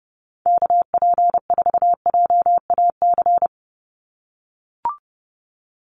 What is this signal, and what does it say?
Morse code, heard as a keyed tone, about 29 wpm, 700 Hz. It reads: KP4JAC
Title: USA Call Signs 30wpm